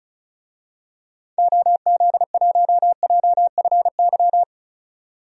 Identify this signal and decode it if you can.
Morse code, heard as a keyed tone, about 34 wpm, 700 Hz. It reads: OZ1JFY
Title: DXCC-100 Call Signs - 6 Characters - 35wpm